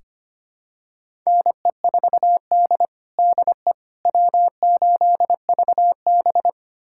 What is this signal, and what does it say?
Morse code, heard as a keyed tone, about 25 wpm, 700 Hz. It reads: NE4D DE W8VB
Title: USA Callsign DE Callsign - 4 Characters - 25wpm